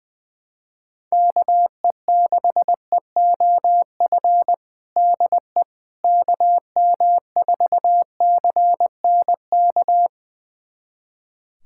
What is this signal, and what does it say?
Morse code, heard as a keyed tone, about 20 wpm, 700 Hz. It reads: KE6EOF DE KM4CNK
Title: USA Callsign DE Callsign - 6 Characters - 20wpm